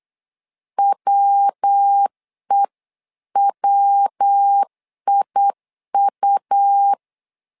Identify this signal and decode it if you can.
Morse code, heard as a keyed tone, about 8 wpm, 800 Hz. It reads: WEWIU
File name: LEZIONI / 3 MWRO / 3 MW 8wpm 2501